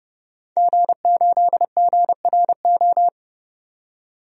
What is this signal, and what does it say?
Morse code, heard as a keyed tone, about 30 wpm, 700 Hz. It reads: G8GRO
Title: DXCC-100 Call Signs - 5 Characters - 30wpm